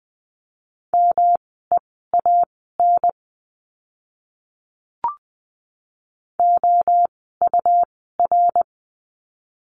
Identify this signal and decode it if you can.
Morse code, heard as a keyed tone, about 20 wpm, spaced as for 15 wpm, 700 Hz. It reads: MEAN OUR
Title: 115 Lesson 19f - Letter w - Words 20wpm